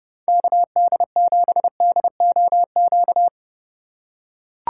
Morse code, heard as a keyed tone, about 31 wpm, 700 Hz. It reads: KD7DOQ